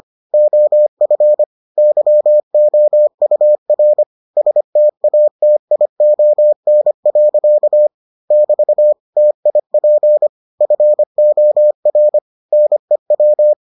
Morse code, heard as a keyed tone, about 25 wpm, 600 Hz. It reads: OF YOUR STATION. <BT> TIP FOR NEW